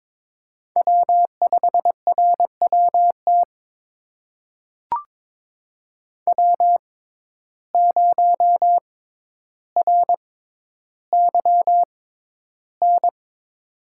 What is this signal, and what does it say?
Morse code, heard as a keyed tone, about 22 wpm, spaced as for 11 wpm, 700 Hz. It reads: W5RWT W0RYN
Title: USA Call Signs - 5 Characters - 6x Character Spacing - 22wpm